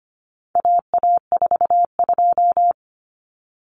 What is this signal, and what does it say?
Morse code, heard as a keyed tone, about 25 wpm, 700 Hz. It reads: AA42
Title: Common Cut Numbers - 4-Digit 25wpm